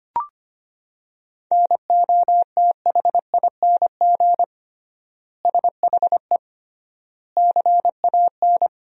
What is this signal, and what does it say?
Morse code, heard as a keyed tone, about 25 wpm, 700 Hz. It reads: NOTHING SHE CAN
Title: 3x Word-Spacing - Sentences from Top 500 Words - 25wpm